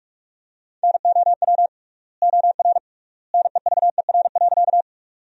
Morse code, heard as a keyed tone, about 46 wpm, 700 Hz. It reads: NOW OR NEVER.
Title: 3x Word-Spacing - Sentences from Top 300 Words - 45wpm